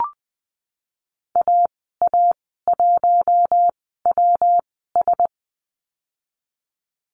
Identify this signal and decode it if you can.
Morse code, heard as a keyed tone, about 20 wpm, spaced as for 15 wpm, 700 Hz. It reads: AA1WS
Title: 173 Lesson 28g - Number 9 - Call Signs 20wpm